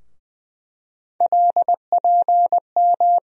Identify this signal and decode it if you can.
Morse code, heard as a keyed tone, about 20 wpm, 700 Hz. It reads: LPM
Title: USA Call Sign Postfixes - 3 Letters 20wpm